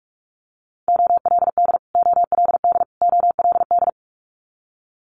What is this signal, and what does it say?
Morse code, heard as a keyed tone, about 46 wpm, 700 Hz. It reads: OLD OLD OLD
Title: Multi-letter Phonemes Repeated 3x 45wpm